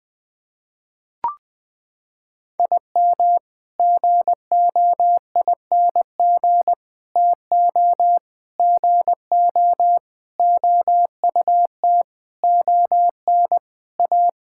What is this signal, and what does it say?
Morse code, heard as a keyed tone, about 21 wpm, 700 Hz. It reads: IM GOING TO GO OUT ON A
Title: Sets of 8 Words - 20wpm